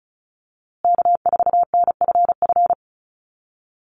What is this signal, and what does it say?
Morse code, heard as a keyed tone, about 34 wpm, 700 Hz. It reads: K4NFF
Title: USA Call Signs Repeated 2x 35wpm